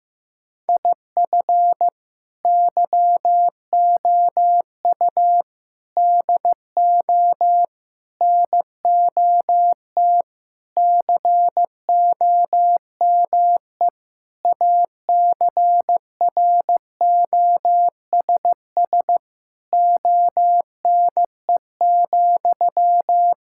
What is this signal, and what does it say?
Morse code, heard as a keyed tone, about 16 wpm, 700 Hz. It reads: IF YOU DO NOT COME ACROSS ONE,
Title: Sentences from Top 1000 Words - Review Part 5 - 15wpm